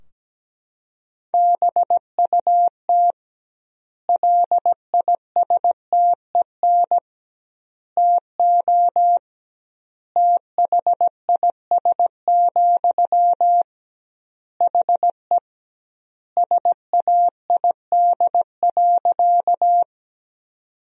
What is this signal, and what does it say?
Morse code, heard as a keyed tone, about 17 wpm, 700 Hz. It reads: BUT LISTEN TO THIS, HE SAID.
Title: 2x Word-Spacing - Sentences from Top 400 Words - 17wpm